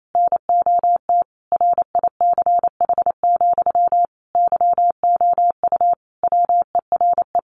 Morse code, heard as a keyed tone, about 28 wpm, 700 Hz. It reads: NOT RICH, YOU WERE
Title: Sentences from Top 1100 Words 28wpm